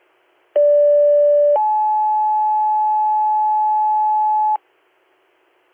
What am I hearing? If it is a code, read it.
Two-tone page, heard as a one-second tone, then a three-second tone, about 580 then 860 Hz, with noise. It is heard from 0.6 to 4.6 s.